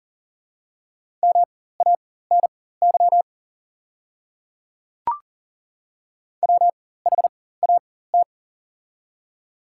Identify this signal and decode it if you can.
Morse code, heard as a keyed tone, about 41 wpm, spaced as for 19 wpm, 690 Hz. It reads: MANY WHAT
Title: 4x Character Spacing - Top 100 Words - 40wpm